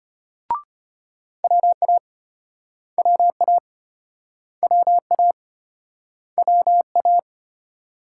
Morse code, heard as a keyed tone, about 34 wpm, 700 Hz. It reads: WA WA WA WA